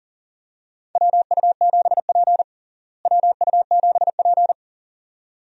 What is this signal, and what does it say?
Morse code, heard as a keyed tone, about 40 wpm, 700 Hz. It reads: WU7P WU7P
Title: USA Call Signs - 4 Characters - Repeated 2x - 40wpm